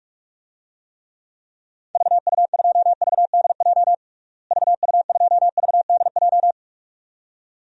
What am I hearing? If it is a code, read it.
Morse code, heard as a keyed tone, about 42 wpm, 700 Hz. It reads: VU2VDJ VU2VDJ
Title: DXCC-100 Call Signs - 6 Characters - Repeated 2x - 45wpm